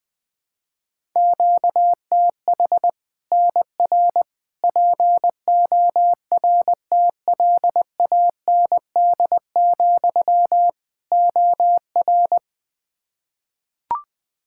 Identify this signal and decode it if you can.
Morse code, heard as a keyed tone, about 20 wpm, 700 Hz. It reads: QTH NR PORTLAND, OR
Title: Practice QSO - Exchange 1 - Beginning - 20wpm